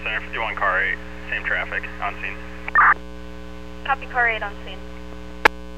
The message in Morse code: none

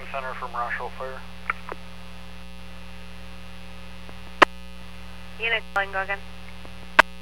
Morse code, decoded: none

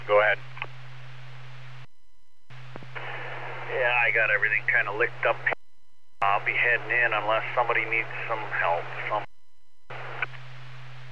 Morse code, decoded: none

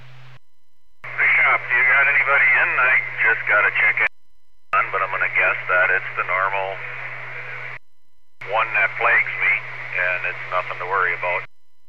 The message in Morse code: none

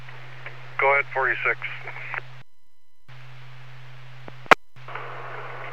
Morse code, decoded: none

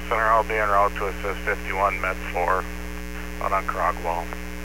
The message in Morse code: none

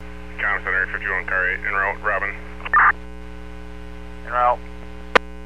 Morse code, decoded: none

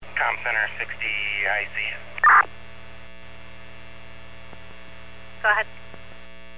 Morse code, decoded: none

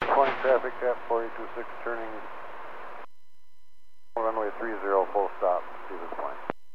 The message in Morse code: none

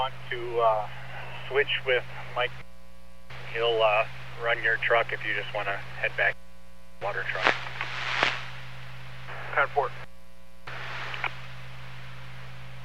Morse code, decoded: none

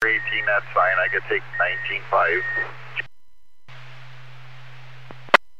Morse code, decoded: none

